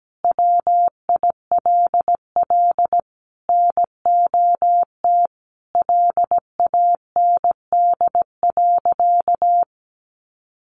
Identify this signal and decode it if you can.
Morse code, heard as a keyed tone, about 17 wpm, 700 Hz. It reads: WILL NOT LAND.
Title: Sentences from Top 1500 Words 17wpm